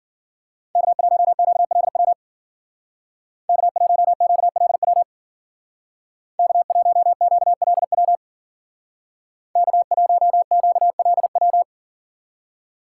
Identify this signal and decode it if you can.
Morse code, heard as a keyed tone, about 53 wpm, 700 Hz. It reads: K1QLW K1QLW K1QLW K1QLW